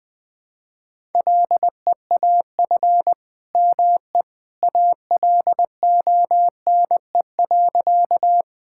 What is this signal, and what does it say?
Morse code, heard as a keyed tone, about 20 wpm, 700 Hz. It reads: LEAF ME ALONE.
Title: All-Age Appropriate Jokes - 20wpm